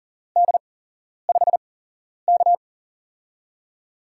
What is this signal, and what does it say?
Morse code, heard as a keyed tone, about 42 wpm, spaced as for 11 wpm, 700 Hz. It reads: D5K
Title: USA Call Signs - 8x Character Spacing - 40wpm